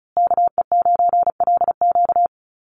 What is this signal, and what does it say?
Morse code, heard as a keyed tone, about 34 wpm, 700 Hz. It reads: KE9LQ
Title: USA Call Signs Repeated 3x 35wpm